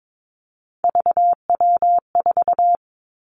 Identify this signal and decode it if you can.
Morse code, heard as a keyed tone, about 22 wpm, 700 Hz. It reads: VW4
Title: Three Challenging Letter-Number Combinations 22wpm